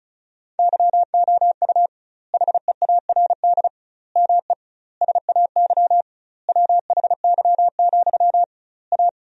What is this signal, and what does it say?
Morse code, heard as a keyed tone, about 36 wpm, 700 Hz. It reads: YOU HEARD ME SAY WHY, A
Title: 2x Word-Spacing - Sentences from Top 500 Words - 35wpm